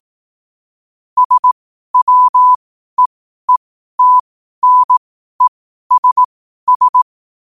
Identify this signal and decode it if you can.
Morse code, heard as a keyed tone, about 18 wpm, spaced as for 12 wpm, 1000 Hz. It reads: SWEETNESS
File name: qotdfiles / 12wpm